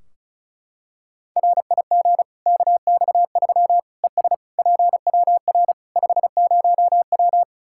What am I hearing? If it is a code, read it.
Morse code, heard as a keyed tone, about 36 wpm, 700 Hz. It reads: RIG KX3 ES PWR 50W
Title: Practice QSO - Exchange 2 - Info - 35wpm